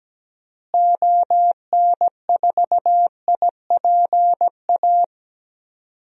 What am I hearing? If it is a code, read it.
Morse code, heard as a keyed tone, about 17 wpm, 700 Hz. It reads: ON4IPA
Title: DXCC-100 Call Signs - 6 Characters - Repeated 3x - 17wpm